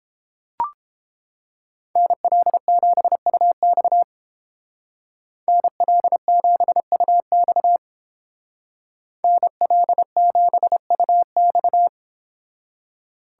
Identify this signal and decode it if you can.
Morse code, heard as a keyed tone, about 30 wpm, 700 Hz. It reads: NL7UX NL7UX NL7UX